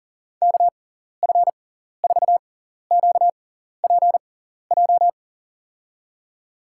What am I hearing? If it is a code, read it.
Morse code, heard as a keyed tone, about 40 wpm, spaced as for 14 wpm, 700 Hz. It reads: KF4QPJ
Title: USA Call Signs - 6 Characters - 6x Character Spacing - 40wpm